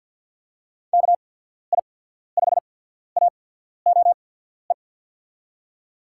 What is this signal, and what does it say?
Morse code, heard as a keyed tone, about 53 wpm, spaced as for 14 wpm, 700 Hz. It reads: KI5AOE
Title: USA Call Signs - 6 Characters - 8x Character Spacing - 50wpm